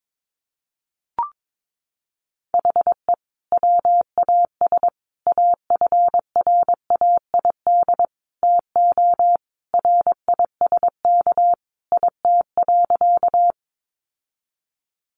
Morse code, heard as a keyed tone, about 22 wpm, 700 Hz. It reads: HE WAS AFRAID TO RISK IT.
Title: Sentences from Top 1500 Words 22wpm